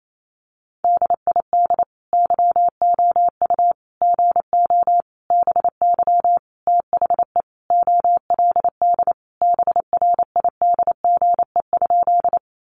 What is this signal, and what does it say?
Morse code, heard as a keyed tone, about 28 wpm, 700 Hz. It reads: DID YOU GO BY THE OLD BRIDGE?
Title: Sentences from Top 1200 Words 28wpm